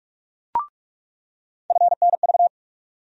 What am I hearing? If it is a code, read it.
Morse code, heard as a keyed tone, about 44 wpm, 700 Hz. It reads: FNV
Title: USA Call Sign Postfixes - 3 Letters 45wpm